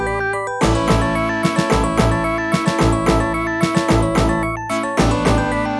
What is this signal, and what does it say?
There is no code.